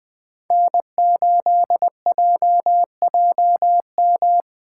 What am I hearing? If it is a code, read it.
Morse code, heard as a keyed tone, about 20 wpm, 700 Hz. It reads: N8JJM
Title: 2023 POTA Hunter Call Signs - 20wpm